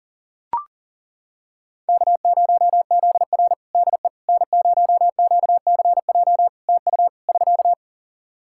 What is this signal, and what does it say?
Morse code, heard as a keyed tone, about 40 wpm, 700 Hz. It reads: K0ZR DE N0QCJ TU <SK>
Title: Practice QSO - Exchange 4 - Ending - 40wpm